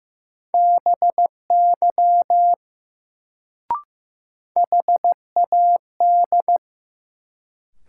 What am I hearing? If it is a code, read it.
Morse code, heard as a keyed tone, about 15 wpm, 700 Hz. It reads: BY HAD
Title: Sentences from Top 100 Words - New Word Drill - 15wpm